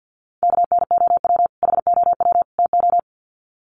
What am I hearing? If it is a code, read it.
Morse code, heard as a keyed tone, about 48 wpm, 700 Hz. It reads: KNOW HOW TO
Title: Sets of 3 Words 50wpm